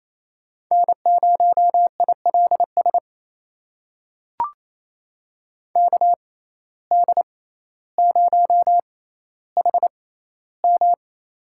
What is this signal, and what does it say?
Morse code, heard as a keyed tone, about 28 wpm, spaced as for 14 wpm, 700 Hz. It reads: N0ILS KD0HM
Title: USA Call Signs - 5 Characters - 6x Character Spacing - 28wpm